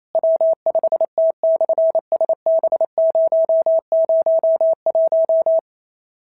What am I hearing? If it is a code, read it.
Morse code, heard as a keyed tone, about 28 wpm, 650 Hz. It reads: W5T/SB001
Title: SOTA QSO Practice 28wpm